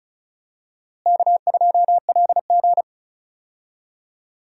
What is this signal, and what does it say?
Morse code, heard as a keyed tone, about 36 wpm, 700 Hz. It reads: K2LG
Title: USA Call Signs - 4 Characters - Repeated 2x - 35wpm